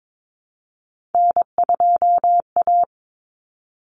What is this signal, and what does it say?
Morse code, heard as a keyed tone, about 22 wpm, 700 Hz. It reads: N2A
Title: Common Cut Numbers - 3-Digit 22wpm